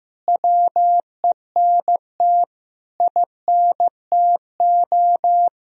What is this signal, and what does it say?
Morse code, heard as a keyed tone, about 15 wpm, 700 Hz. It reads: WENT INTO
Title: Sentences from Top 1000 Words - Review Part 2 - 15wpm